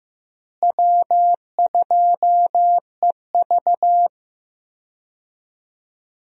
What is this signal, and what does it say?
Morse code, heard as a keyed tone, about 15 wpm, 700 Hz. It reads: W2EV
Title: USA Call Signs - 4 Characters - Repeated 2x - 15wpm